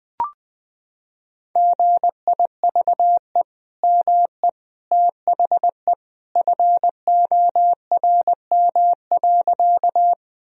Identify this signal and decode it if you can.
Morse code, heard as a keyed tone, about 21 wpm, 700 Hz. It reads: GIVE ME THE FORM.
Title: Sentences from Top 1000 Words - Review Part 1 - 20wpm